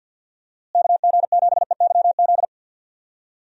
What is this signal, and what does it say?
Morse code, heard as a keyed tone, about 50 wpm, 700 Hz. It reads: KG7EYZ
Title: USA Call Signs - 6 Characters - Repeated 3x - 50wpm